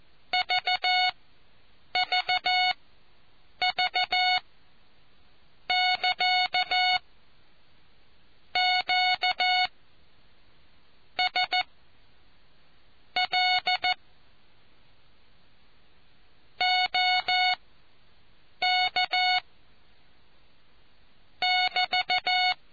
Morse code, heard as a keyed tone, about 14 wpm, spaced as for 7 wpm, 730 Hz. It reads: VVV<KA> Q S L OK <BT>